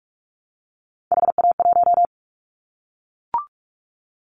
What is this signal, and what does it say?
Morse code, heard as a keyed tone, about 44 wpm, 700 Hz. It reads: HA1
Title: Callsign Prefixes for Top 25 DXCC - Repeated 3x - 45wpm